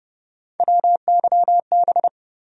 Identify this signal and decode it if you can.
Morse code, heard as a keyed tone, about 30 wpm, 700 Hz. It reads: WYB
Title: USA Call Sign Postfixes - 3 Letters 30wpm